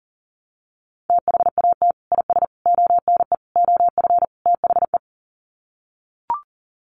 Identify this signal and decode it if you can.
Morse code, heard as a keyed tone, about 37 wpm, 700 Hz. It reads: THAT IS ONE OF THE
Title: Sets of 5 Words Encore 40wpm